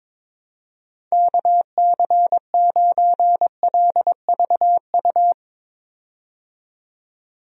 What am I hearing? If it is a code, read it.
Morse code, heard as a keyed tone, about 22 wpm, 700 Hz. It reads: KC9LVU
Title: USA Call Signs - 6 Characters - 22wpm